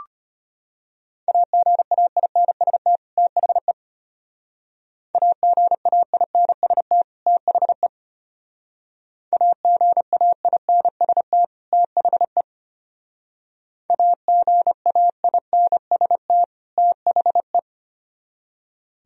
Morse code, heard as a keyed tone, about 32 wpm, 700 Hz. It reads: AGAINST THE AGAINST THE AGAINST THE AGAINST THE